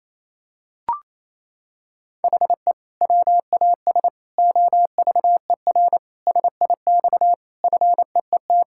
Morse code, heard as a keyed tone, about 28 wpm, 700 Hz. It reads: HE WAS OVER SIX FEET
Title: Sentences from Top 1600 Words 28wpm